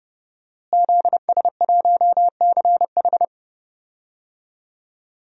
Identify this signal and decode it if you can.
Morse code, heard as a keyed tone, about 32 wpm, 700 Hz. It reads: ZS1CH
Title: DXCC-100 Call Signs - 5 Characters - Repeated 3x - 30wpm